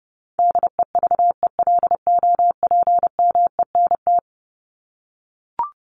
Morse code, heard as a keyed tone, about 30 wpm, 700 Hz. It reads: DEVELOPMENT
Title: Top 501-750 Words 30wpm